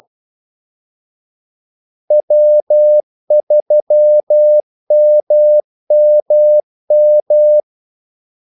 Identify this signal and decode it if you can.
Morse code, heard as a keyed tone, about 12 wpm, 600 Hz. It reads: W3MMM
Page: 2 ekstra ordmellemrum mellem kaldesignalerne